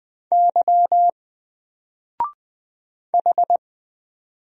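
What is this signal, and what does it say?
Morse code, heard as a keyed tone, about 20 wpm, 700 Hz. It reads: Y H
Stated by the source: Single letter and number Morse drill at 20 wpm, from N4LCD.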